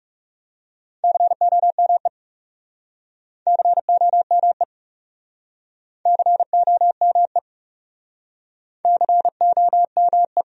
Morse code, heard as a keyed tone, about 39 wpm, 700 Hz. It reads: COME COME COME COME